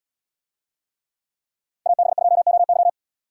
Morse code, heard as a keyed tone, about 77 wpm, 700 Hz. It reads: AB2OY